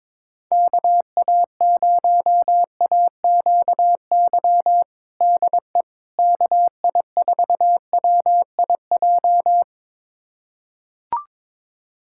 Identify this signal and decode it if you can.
Morse code, heard as a keyed tone, about 21 wpm, 700 Hz. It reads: KA0AQY DE KI4WIJ
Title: USA Callsign DE Callsign - 6 Characters - 22wpm